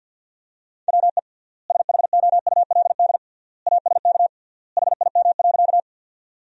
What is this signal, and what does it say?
Morse code, heard as a keyed tone, about 49 wpm, 700 Hz. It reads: WE SHOULD ASK HIM.
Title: 3x Word-Spacing - Sentences from Top 300 Words - 50wpm